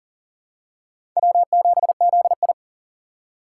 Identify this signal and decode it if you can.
Morse code, heard as a keyed tone, about 40 wpm, 700 Hz. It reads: W7ZI
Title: USA Call Signs - 4 Characters - 2x Character Spacing - 40wpm